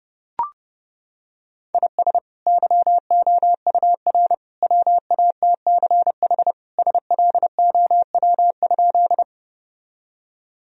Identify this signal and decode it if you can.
Morse code, heard as a keyed tone, about 30 wpm, 700 Hz. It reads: IS YOUR WATCH SLOW?
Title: Sentences from Top 1000 Words - Review Part 2 - 30wpm